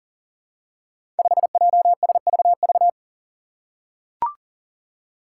Morse code, heard as a keyed tone, about 37 wpm, 700 Hz. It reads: 5JSVV
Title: Five Challenging Letter-Number Combinations 40wpm